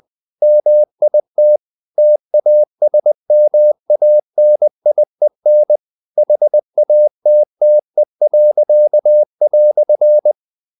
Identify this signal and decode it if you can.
Morse code, heard as a keyed tone, about 20 wpm, 600 Hz. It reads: MIT TASMANIEN HATTE."
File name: dk5ke / Humor-100BpM